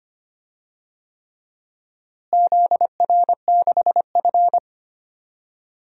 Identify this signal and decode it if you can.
Morse code, heard as a keyed tone, about 25 wpm, 700 Hz. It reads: ZR6F
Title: DXCC-100 Call Signs - 4 Characters - Repeated 3x - 25wpm